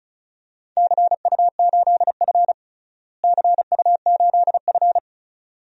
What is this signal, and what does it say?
Morse code, heard as a keyed tone, about 36 wpm, 700 Hz. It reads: CU8F CU8F
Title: DXCC-100 Call Signs - 4 Characters - Repeated 2x - 35wpm